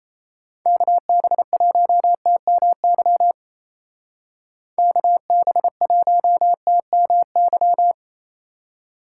Morse code, heard as a keyed tone, about 34 wpm, 700 Hz. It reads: KB1TMY KB1TMY
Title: Speed Racing - USA 2x3 Call Signs 28wpm